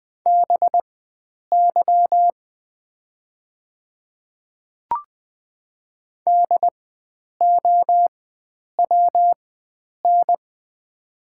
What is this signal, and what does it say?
Morse code, heard as a keyed tone, about 21 wpm, spaced as for 9 wpm, 700 Hz. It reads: BY DOWN
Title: 4x Character Spacing - Top 100 Words - 20wpm